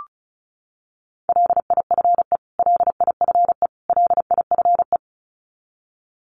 Morse code, heard as a keyed tone, about 34 wpm, 700 Hz. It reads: LIFE LIFE LIFE
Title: Top 500 Words Repeated 35wpm